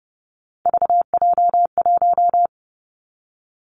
Morse code, heard as a keyed tone, about 30 wpm, 700 Hz. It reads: VJ1
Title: Three Challenging Letter-Number Combinations 30wpm